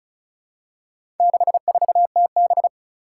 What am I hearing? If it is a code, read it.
Morse code, heard as a keyed tone, about 36 wpm, 700 Hz. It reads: 64TB
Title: Four Challenging Letter-Number Combinations 35wpm